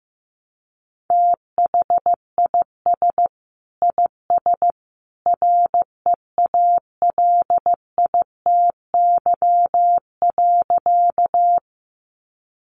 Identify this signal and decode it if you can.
Morse code, heard as a keyed tone, about 15 wpm, 700 Hz. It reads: THIS IS REALITY.
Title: Sentences from Top 1200 Words 15wpm